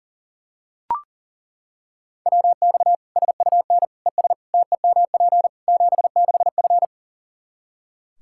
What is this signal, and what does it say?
Morse code, heard as a keyed tone, about 40 wpm, 700 Hz. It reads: WX SUN ES TEMP 76F
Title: Practice QSO - Exchange 2 - Info - 40wpm